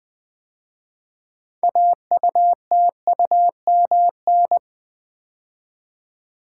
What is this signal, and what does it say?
Morse code, heard as a keyed tone, about 20 wpm, 700 Hz. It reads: AUTUMN
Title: 6 Letter Words - First 3 Letters Given - 20wpm